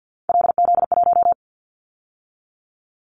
Morse code, heard as a keyed tone, about 50 wpm, 700 Hz. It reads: LZ1
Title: Callsign Prefixes for Top 50 DXCC - 50wpm